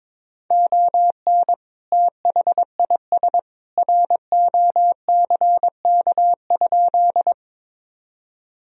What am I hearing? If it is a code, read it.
Morse code, heard as a keyed tone, about 21 wpm, 700 Hz. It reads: ON THIS ROCK?
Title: Sentences from Top 1000 Words - Review Part 2 - 22wpm